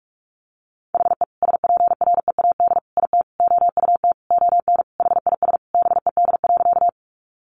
Morse code, heard as a keyed tone, about 44 wpm, 700 Hz. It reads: HE SPREAD IT OUT ON HIS BED.
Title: Sentences from Top 1200 Words 45wpm